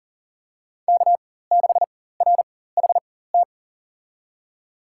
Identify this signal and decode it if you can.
Morse code, heard as a keyed tone, about 39 wpm, spaced as for 19 wpm, 700 Hz. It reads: K6RHT
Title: USA Call Signs - 4x Character Spacing - 40wpm